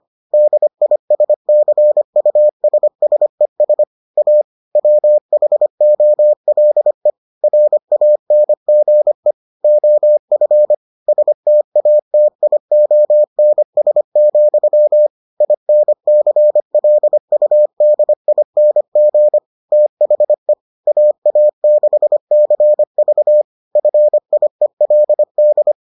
Morse code, heard as a keyed tone, about 25 wpm, 600 Hz. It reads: DISCUSSES A WHOLE RANGE OF STATIONS, INCLUDING THE AA6CV FIELD